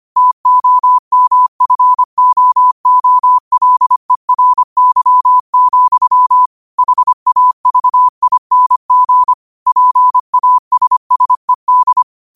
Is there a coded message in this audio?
Yes: Morse code from 0.2 to 12.0 s, heard as a keyed tone, about 25 wpm, 1000 Hz. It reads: TOMFOOLERY, HAVING PASSED